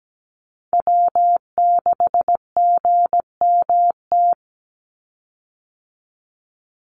Morse code, heard as a keyed tone, about 17 wpm, 700 Hz. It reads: W6GMT
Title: Call Signs - 17wpm